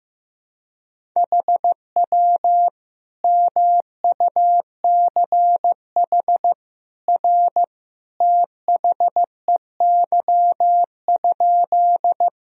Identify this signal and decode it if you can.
Morse code, heard as a keyed tone, about 15 wpm, 700 Hz. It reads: HW MUCH R THEY?
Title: Sentences from Top 200 Words and Abbreviations 15wpm